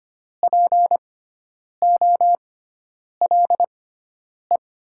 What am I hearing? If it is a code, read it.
Morse code, heard as a keyed tone, about 25 wpm, spaced as for 9 wpm, 690 Hz. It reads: POLE
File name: Top 90 QSO Elements - 6x Character Spacing - 25wpm